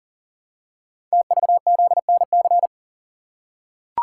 Morse code, heard as a keyed tone, about 38 wpm, 690 Hz. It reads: TVZNC
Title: Code Groups - 5 Letter:Numbers - 40wpm